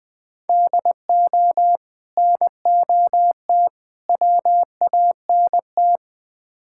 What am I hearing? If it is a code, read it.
Morse code, heard as a keyed tone, about 20 wpm, 700 Hz. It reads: DO NOT WANT
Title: Build A Phrase - 5-Word Phrases - 20wpm